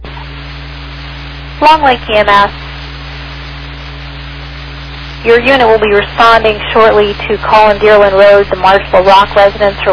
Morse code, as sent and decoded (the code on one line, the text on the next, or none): none